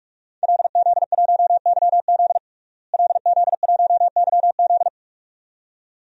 Morse code, heard as a keyed tone, about 46 wpm, 700 Hz. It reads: LZ1YZ LZ1YZ
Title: DXCC-100 Call Signs - 5 Characters - Repeated 2x - 45wpm